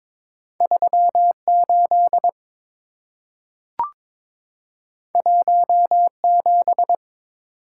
Morse code, heard as a keyed tone, about 22 wpm, 690 Hz. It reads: 38 17
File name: practice / 22wpm / Two Letter-Number Combinations - 22wpm